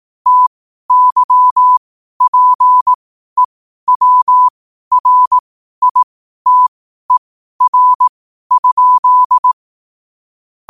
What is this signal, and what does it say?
Morse code, heard as a keyed tone, about 18 wpm, spaced as for 12 wpm, 1000 Hz. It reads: TYPEWRITER?
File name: qotdfiles / 12wpm